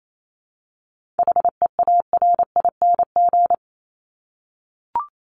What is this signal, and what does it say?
Morse code, heard as a keyed tone, about 28 wpm, 700 Hz. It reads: HEARING
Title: Top 1751-2000 Words 28wpm